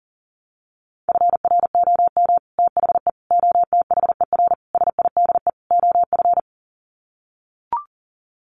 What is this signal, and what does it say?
Morse code, heard as a keyed tone, about 40 wpm, 700 Hz. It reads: FROM THE OTHER SIDE OF
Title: Sets of 5 Words Encore 40wpm